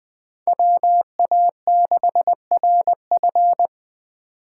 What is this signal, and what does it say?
Morse code, heard as a keyed tone, about 20 wpm, 700 Hz. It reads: WA6RF
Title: USA Call Signs - 5 Characters - 3x Character Spacing - 20wpm